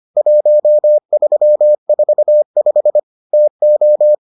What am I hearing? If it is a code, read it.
Morse code, heard as a keyed tone, about 25 wpm, 600 Hz. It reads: 1345 TO